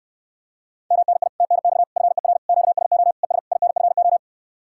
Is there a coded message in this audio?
Yes: Morse code from 0.9 to 4.2 s, heard as a keyed tone, about 69 wpm, 700 Hz. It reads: MNI TNX FER QSO ES INFO